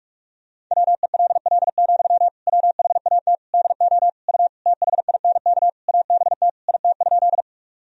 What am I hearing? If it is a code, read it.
Morse code, heard as a keyed tone, about 46 wpm, 700 Hz. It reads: WELL, WHAT DO U THINK ABT IT?
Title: Sentences from Top 200 Words and Abbreviations 45wpm